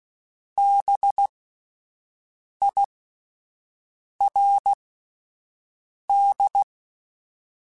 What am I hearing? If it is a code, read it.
Morse code, heard as a keyed tone, about 16 wpm, spaced as for 5 wpm, 800 Hz. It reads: BIRD
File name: General / General-mp3s / Houses @10